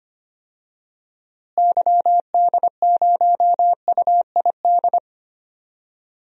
Morse code, heard as a keyed tone, about 25 wpm, 700 Hz. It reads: YD0UID
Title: DXCC-100 Call Signs - 6 Characters - Repeated 3x - 25wpm